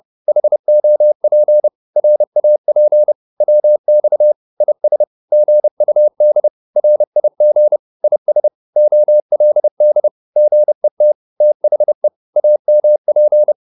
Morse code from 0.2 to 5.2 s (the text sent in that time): HOP RAP WX IS